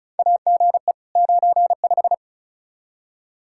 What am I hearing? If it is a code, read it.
Morse code, heard as a keyed tone, about 36 wpm, 700 Hz. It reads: AGE 95
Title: K1USN SST QSO Practice 35wpm